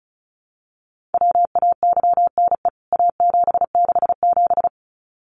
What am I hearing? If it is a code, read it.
Morse code, heard as a keyed tone, about 34 wpm, 700 Hz. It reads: WAYNE A767
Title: CWT Exchange 35wpm